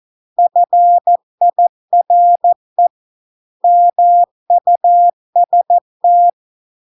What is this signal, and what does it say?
Morse code, heard as a keyed tone, about 14 wpm, 700 Hz. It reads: FIRE MUST